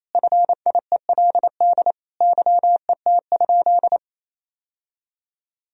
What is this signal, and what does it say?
Morse code, heard as a keyed tone, about 28 wpm, 700 Hz. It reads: FIELD YET?
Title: Sentences from Top 1000 Words - Review Part 2 - 28wpm